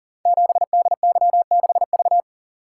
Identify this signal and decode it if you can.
Morse code, heard as a keyed tone, about 38 wpm, 690 Hz. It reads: 7DY6V
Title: Code Groups - 5 Letter:Numbers - 40wpm